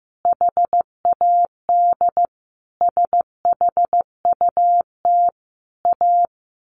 Morse code, heard as a keyed tone, about 15 wpm, 700 Hz. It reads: HAD SHUT A
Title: A Sound Of Thunder 15wpm